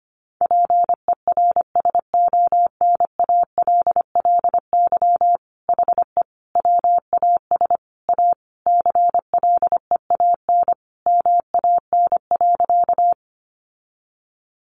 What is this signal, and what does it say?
Morse code, heard as a keyed tone, about 25 wpm, 700 Hz. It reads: PERSONALLY HE WAS A CLEAN MAN.
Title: Sentences from Top 1400 Words 25wpm